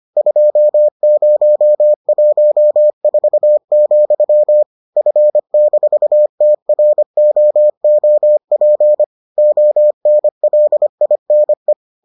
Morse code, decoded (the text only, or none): 2014, F-TROOP ONLINE